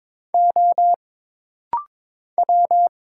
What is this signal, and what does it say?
Morse code, heard as a keyed tone, about 22 wpm, 700 Hz. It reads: O W
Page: This rapid-fire timing is geared to push your ICR (Instant Character Recognition) skills.